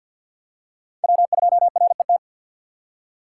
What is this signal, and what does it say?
Morse code, heard as a keyed tone, about 48 wpm, 700 Hz. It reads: W2RET
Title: USA Call Signs - 6x Character Spacing - 50wpm